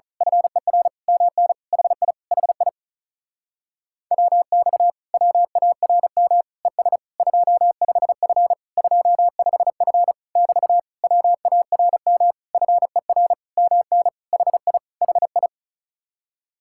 Morse code, heard as a keyed tone, about 33 wpm, 700 Hz. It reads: FER MN HI HI WX WARM ES 25F 25F <BT> WARM FER MN HI HI
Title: Speed-Racing Common QSO Phrases Encore - 35wpm